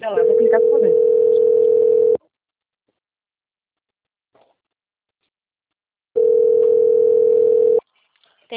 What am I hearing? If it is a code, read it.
Ringback tone (North America), 440 + 480 Hz, with speech.
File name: out- - - - - .2653